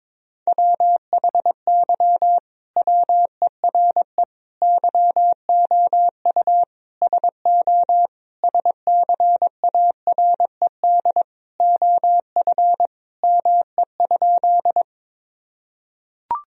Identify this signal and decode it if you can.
Morse code, heard as a keyed tone, about 22 wpm, 700 Hz. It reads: WHY WERE YOU SO SCARED OF ME?
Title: Sentences from Top 1000 Words - Review Part 4 - 22wpm